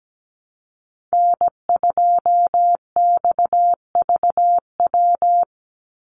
Morse code, heard as a keyed tone, about 17 wpm, 700 Hz. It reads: N2XVW
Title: USA Call Signs 17wpm